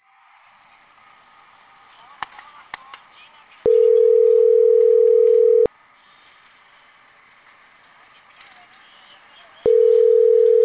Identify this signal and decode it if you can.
Ringback tone (North America), 440 + 480 Hz.
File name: CENTRAL CAEA / out agcaea02